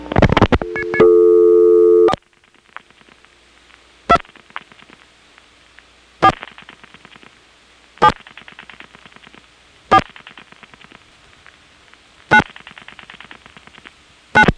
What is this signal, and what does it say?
DTMF, heard as telephone touch tones, tones of about 70 ms. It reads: 4 2 4 7 4 9 9
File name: phonetrips / SmyrnaGA3HQ